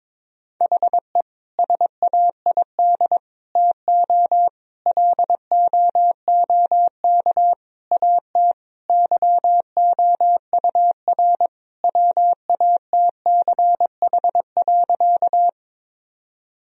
Morse code, heard as a keyed tone, about 22 wpm, 700 Hz. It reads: HE SAID TO LOOK AT YOUR WATCH.
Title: Sentences from Top 1000 Words - Review Part 1 - 22wpm